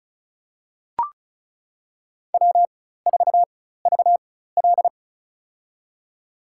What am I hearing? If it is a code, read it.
Morse code, heard as a keyed tone, about 36 wpm, spaced as for 16 wpm, 700 Hz. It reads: W4VL
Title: USA Call Signs - 4x Character Spacing - 35wpm